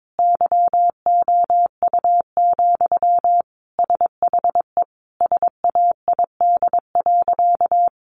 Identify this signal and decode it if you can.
Morse code, heard as a keyed tone, about 22 wpm, 700 Hz. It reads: YOU, SHE SAID.
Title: Sentences from Top 600 Words 22wpm